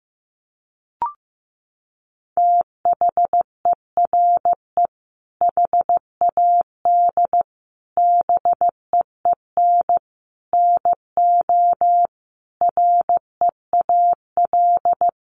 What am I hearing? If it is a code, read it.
Morse code, heard as a keyed tone, about 15 wpm, 700 Hz. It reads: THERE HAD BEEN NO REAL
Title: Sentences from Top 1400 Words 15wpm